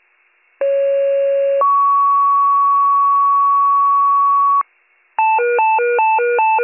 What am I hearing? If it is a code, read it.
Two-tone page, heard as a one-second tone, then a three-second tone, about 550 then 1120 Hz. It is heard from 0.6 to 4.6 s.